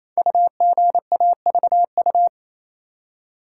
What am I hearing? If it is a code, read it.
Morse code, heard as a keyed tone, about 28 wpm, 700 Hz. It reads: UGAVU
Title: Code Groups - 5 Letter:Numbers - 28wpm